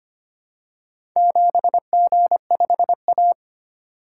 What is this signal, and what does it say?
Morse code, heard as a keyed tone, about 25 wpm, 700 Hz. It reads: 7G5A
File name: practice / 25wpm / Four Letter-Number Combinations - 25wpm